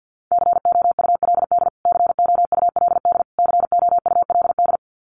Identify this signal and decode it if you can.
Morse code, heard as a keyed tone, about 50 wpm, 700 Hz. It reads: COULD COULD COULD
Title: Top 500 Words Repeated 50wpm